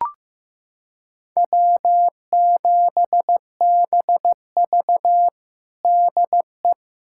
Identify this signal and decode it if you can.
Morse code, heard as a keyed tone, about 16 wpm, 700 Hz. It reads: W7BV DE
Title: USA Callsign DE Callsign - 4 Characters - 15wpm